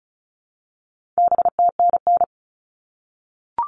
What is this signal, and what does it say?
Morse code, heard as a keyed tone, about 34 wpm, 700 Hz. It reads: BTNN
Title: All Cut Numbers - 4-Digit 35wpm